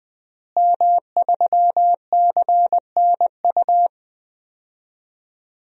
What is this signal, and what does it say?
Morse code, heard as a keyed tone, about 20 wpm, 700 Hz. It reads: M3CNU
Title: DXCC-100 Call Signs - 5 Characters - Repeated 3x - 20wpm